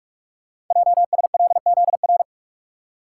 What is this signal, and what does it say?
Morse code, heard as a keyed tone, about 44 wpm, 700 Hz. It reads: JSLZR
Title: Code Groups - 5 Letters - 45wpm